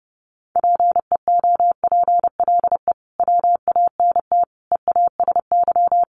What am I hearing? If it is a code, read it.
Morse code, heard as a keyed tone, about 30 wpm, 700 Hz. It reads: PEOPLE WANT EASY
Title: Sentences from Top 1500 Words 30wpm